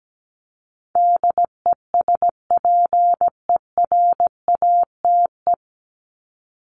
Morse code, heard as a keyed tone, about 17 wpm, 700 Hz. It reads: DESPERATE
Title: Top 1751-2000 Words 17wpm